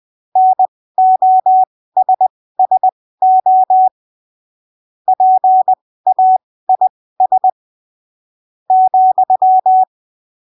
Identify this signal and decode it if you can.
Morse code, heard as a keyed tone, about 20 wpm, spaced as for 15 wpm, 750 Hz. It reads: NOSSO PAIS ,
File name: arquivos / texto / texto 22 20ppm